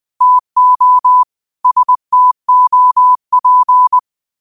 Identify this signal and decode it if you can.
Morse code, heard as a keyed tone, about 20 wpm, 1000 Hz. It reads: TO STOP
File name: qotdfiles / 20wpm